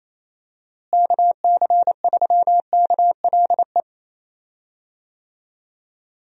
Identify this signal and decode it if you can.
Morse code, heard as a keyed tone, about 27 wpm, 700 Hz. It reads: KC3KLE
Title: USA Call Signs - 6 Characters - Repeated 3x - 28wpm